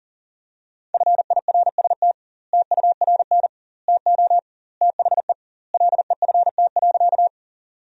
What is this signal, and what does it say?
Morse code, heard as a keyed tone, about 40 wpm, 700 Hz. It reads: FIRST TURN TO THE LEFT.
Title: 2x Word-Spacing - Sentences from Top 300 Words - 40wpm